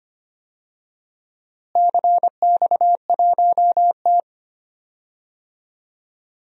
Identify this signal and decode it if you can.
Morse code, heard as a keyed tone, about 25 wpm, 700 Hz. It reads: CX1T
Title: DXCC-100 Call Signs - 4 Characters - Repeated 2x - 25wpm